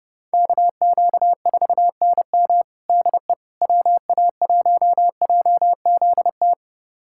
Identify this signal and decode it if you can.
Morse code, heard as a keyed tone, about 30 wpm, 700 Hz. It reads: KQ4NM DE WA1JZT
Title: USA Callsign DE Callsign 30wpm